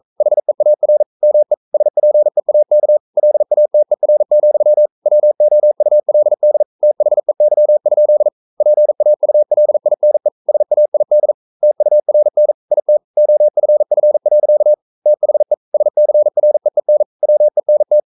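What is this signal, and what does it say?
Morse code, heard as a keyed tone, about 42 wpm, 600 Hz. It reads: HEAR ME SPEAK LATER, WOULD THEY? PAULINE SAID TURN IT OFF. THE SCREEN WENT
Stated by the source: Morse code practice at 41 wpm, from CW med Gnister.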